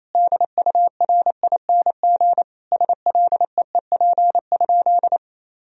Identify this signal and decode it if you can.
Morse code, heard as a keyed tone, about 27 wpm, 700 Hz. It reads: DURING SLEEP?
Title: Sentences from Top 1000 Words - Review Part 2 - 28wpm